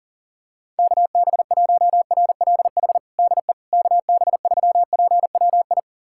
Morse code, heard as a keyed tone, about 40 wpm, 700 Hz. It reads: KB1RLH DE KB3PWI
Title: USA Callsign DE Callsign 40wpm